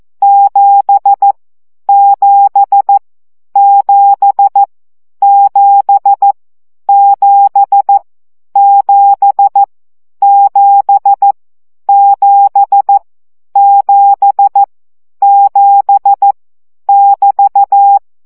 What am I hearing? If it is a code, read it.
Morse code, heard as a keyed tone, about 14 wpm, 800 Hz. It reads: 7 7 7 7 7 7 7 7 7 7 <BT>
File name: CW gyakorlas 7 12 baud